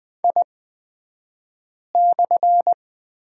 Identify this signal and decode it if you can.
Morse code, heard as a keyed tone, about 20 wpm, spaced as for 5 wpm, 700 Hz. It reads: I/
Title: 228 Lesson 37d - Letter x - Two Characters 20wpm